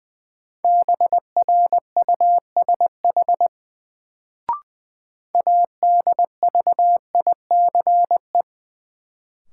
Morse code, heard as a keyed tone, about 21 wpm, 700 Hz. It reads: BRUSH ADVICE
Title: Sentences from Top 1100 Words - New Word Drill Alt - 20wpm